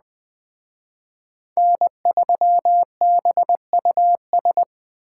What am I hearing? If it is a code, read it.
Morse code, heard as a keyed tone, about 20 wpm, 700 Hz. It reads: N3BUS
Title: Speed Racing - USA Call Signs 17wpm